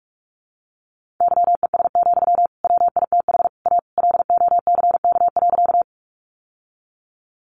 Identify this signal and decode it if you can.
Morse code, heard as a keyed tone, about 44 wpm, 700 Hz. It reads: YES, WITH A LOCK.
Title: Sentences from Top 1300 Words 45wpm